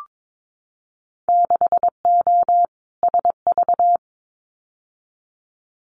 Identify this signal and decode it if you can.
Morse code, heard as a keyed tone, about 22 wpm, 700 Hz. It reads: 6O SV
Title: Winter Field Day Exchanges Encore 22wpm